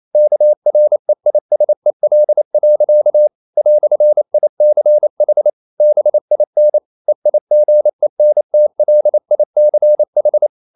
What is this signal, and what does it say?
Morse code, heard as a keyed tone, about 28 wpm, 600 Hz. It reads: KREISEL. "ICH BIN EIGENTLICH
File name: Liebespaar4 28wpm